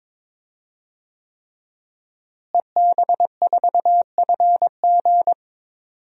Morse code, heard as a keyed tone, about 22 wpm, 700 Hz. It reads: EB4FG
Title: DXCC-100 Call Signs - 5 Characters - Repeated 3x - 22wpm